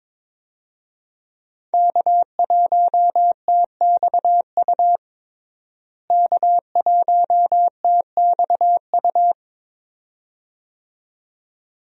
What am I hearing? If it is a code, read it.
Morse code, heard as a keyed tone, about 22 wpm, 700 Hz. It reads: K1TXU K1TXU
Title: USA Call Signs - 5 Characters - Repeated 2x - 22wpm